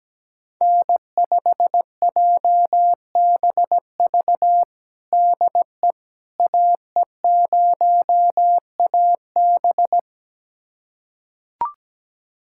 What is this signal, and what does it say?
Morse code, heard as a keyed tone, about 17 wpm, 700 Hz. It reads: N5JBV DE AE0AB
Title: USA Callsign DE Callsign - 5 Characters - 17wpm